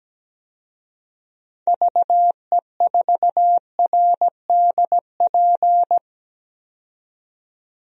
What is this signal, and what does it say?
Morse code, heard as a keyed tone, about 17 wpm, 700 Hz. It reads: VE4RDP
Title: DXCC-100 Call Signs - 6 Characters - 17wpm